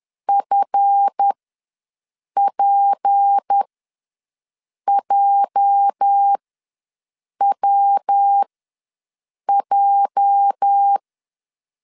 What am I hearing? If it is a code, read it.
Morse code, heard as a keyed tone, about 10 wpm, spaced as for 6 wpm, 800 Hz. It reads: FPJWJ
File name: LEZIONI / 8 YJ45 / 8 YJ ws 10wpm 4043